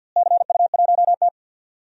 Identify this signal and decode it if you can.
Morse code, heard as a keyed tone, about 52 wpm, 700 Hz. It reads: CU1T
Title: DXCC-100 Call Signs - 4 Characters - Repeated 3x - 50wpm